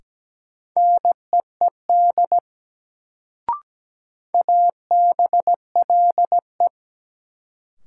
Morse code, heard as a keyed tone, about 17 wpm, 700 Hz. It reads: NEED ABLE
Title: Sets of 2 Words - New Word Drill - 17wpm